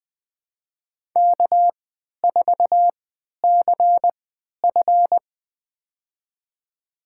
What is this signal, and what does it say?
Morse code, heard as a keyed tone, about 20 wpm, spaced as for 11 wpm, 700 Hz. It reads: K4CF
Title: USA Call Signs - 3x Character Spacing - 20wpm